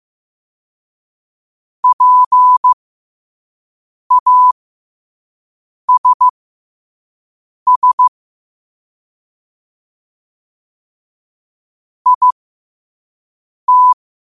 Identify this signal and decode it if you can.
Morse code, heard as a keyed tone, about 15 wpm, spaced as for 5 wpm, 1000 Hz. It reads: PASS IT
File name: qotdfiles / 5wpm